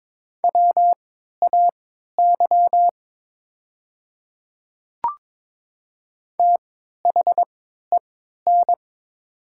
Morse code, heard as a keyed tone, about 22 wpm, spaced as for 13 wpm, 700 Hz. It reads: WAY THEN
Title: 3x Character Spacing - Top 500 Words - 22wpm